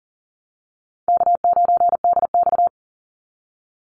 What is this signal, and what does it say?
Morse code, heard as a keyed tone, about 42 wpm, 700 Hz. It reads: K9DX
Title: Easy Call Signs - 40wpm